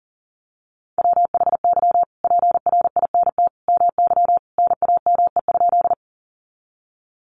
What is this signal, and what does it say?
Morse code, heard as a keyed tone, about 40 wpm, 700 Hz. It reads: WHY PRINT MY NAME?
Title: Sentences from Top 1400 Words 40wpm